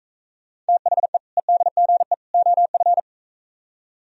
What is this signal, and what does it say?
Morse code, heard as a keyed tone, about 43 wpm, 700 Hz. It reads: THE EDGE OF